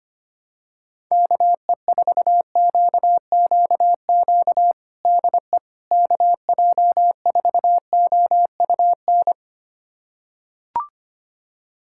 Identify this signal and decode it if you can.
Morse code, heard as a keyed tone, about 25 wpm, 700 Hz. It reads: KE4QQQ DE KJ4OUN
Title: USA Callsign DE Callsign - 6 Characters - 25wpm